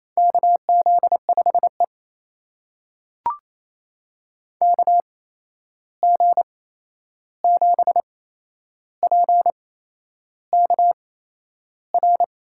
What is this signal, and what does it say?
Morse code, heard as a keyed tone, about 28 wpm, spaced as for 11 wpm, 700 Hz. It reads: KZ5E KG7PKR
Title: USA Call Signs - 8x Character Spacing - 28wpm